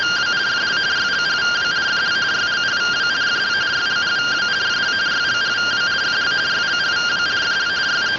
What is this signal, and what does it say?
Two-tone teleprinter signal, about 170 Hz shift.